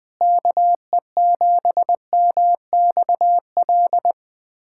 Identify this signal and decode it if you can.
Morse code, heard as a keyed tone, about 20 wpm, 700 Hz. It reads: KE7MXL
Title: USA Callsign DE Callsign 20wpm